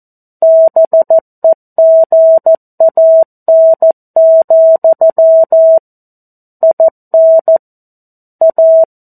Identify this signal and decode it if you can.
Morse code, heard as a keyed tone, about 14 wpm, 650 Hz. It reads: BEGAN, IN A
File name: 02-Chapter-02 14 WPM